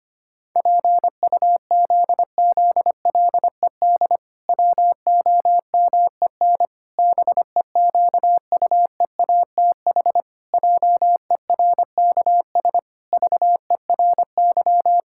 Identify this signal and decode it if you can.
Morse code, heard as a keyed tone, about 25 wpm, 700 Hz. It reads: PUZZLED WOMEN BEQUEATH JERKS VERY
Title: Pangram Challenge v2 - 25wpm